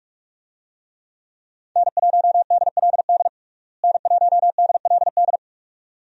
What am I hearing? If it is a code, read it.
Morse code, heard as a keyed tone, about 43 wpm, 700 Hz. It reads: N1DLD N1DLD
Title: USA Call Signs - 5 Characters - Repeated 2x - 45wpm